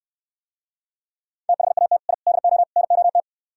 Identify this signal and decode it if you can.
Morse code, heard as a keyed tone, about 68 wpm, 700 Hz. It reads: THAT I DO NOT